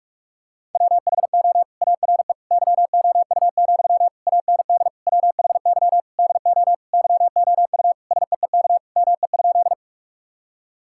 Morse code, heard as a keyed tone, about 46 wpm, 700 Hz. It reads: WHO ARE YOU, AND WHY DO YOU SEEK ME?
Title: Sentences from Top 1600 Words 45wpm